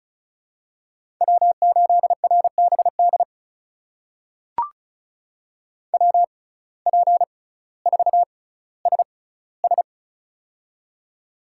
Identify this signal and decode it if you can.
Morse code, heard as a keyed tone, about 36 wpm, spaced as for 18 wpm, 700 Hz. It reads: W8RBD WP4SS
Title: USA Call Signs - 5 Characters - 6x Character Spacing - 35wpm